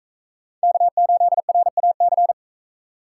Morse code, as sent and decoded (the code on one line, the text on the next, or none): -.- ---.. .-. .- -.-.
K8RAC